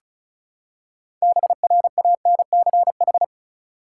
Morse code, heard as a keyed tone, about 36 wpm, 700 Hz. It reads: BRANCH
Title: Six Letter Words - 35wpm